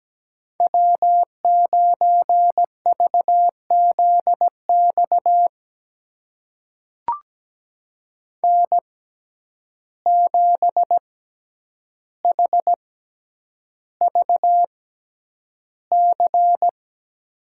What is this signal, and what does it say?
Morse code, heard as a keyed tone, about 17 wpm, spaced as for 9 wpm, 700 Hz. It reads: W9VZX N7HVC
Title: USA Call Signs - 5 Characters - 6x Character Spacing - 17wpm